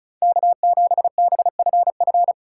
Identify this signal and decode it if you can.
Morse code, heard as a keyed tone, about 36 wpm, 700 Hz. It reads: K7BFF
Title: USA Call Signs - 4x Character Spacing - 35wpm